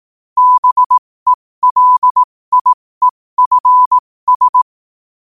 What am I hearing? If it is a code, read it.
Morse code, heard as a keyed tone, about 18 wpm, spaced as for 15 wpm, 1000 Hz. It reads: BELIEFS
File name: qotdfiles / 15wpm